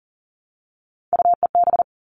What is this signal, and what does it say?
Morse code, heard as a keyed tone, about 40 wpm, 700 Hz. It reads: UEB
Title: All Cut Numbers - 3-Digit 40wpm